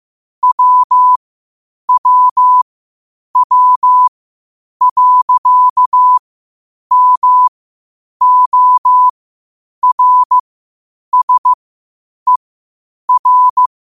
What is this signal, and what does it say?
Morse code, heard as a keyed tone, about 15 wpm, spaced as for 8 wpm, 1000 Hz. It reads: WWW.MORSER